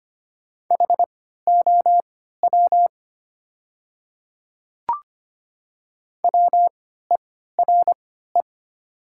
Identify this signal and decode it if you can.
Morse code, heard as a keyed tone, about 25 wpm, spaced as for 14 wpm, 700 Hz. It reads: HOW WERE
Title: 3x Character Spacing - Top 100 Words - 25wpm